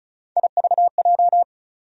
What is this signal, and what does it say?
Morse code, heard as a keyed tone, about 36 wpm, 700 Hz. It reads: IVJ
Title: USA Call Sign Postfixes - 3 Letters 35wpm